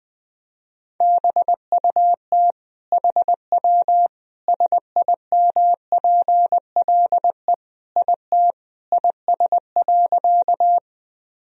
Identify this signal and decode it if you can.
Morse code, heard as a keyed tone, about 20 wpm, 700 Hz. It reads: BUT HW SIMPLE IT IS.
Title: Sentences from Top 400 Words and Abbreviations 20wpm